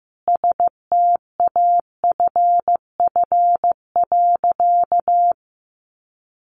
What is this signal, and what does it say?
Morse code, heard as a keyed tone, about 15 wpm, 700 Hz. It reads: STAFF.
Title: Sentences from Top 1200 Words 15wpm